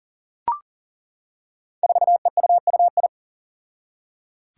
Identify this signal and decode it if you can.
Morse code, heard as a keyed tone, about 40 wpm, 700 Hz. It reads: 4EUUI
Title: Five Challenging Letter-Number Combinations 40wpm